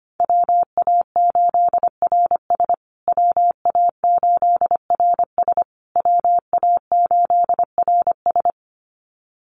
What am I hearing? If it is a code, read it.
Morse code, heard as a keyed tone, about 25 wpm, 700 Hz. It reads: WA8RS WA8RS WA8RS
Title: USA Call Signs Repeated 3x 25wpm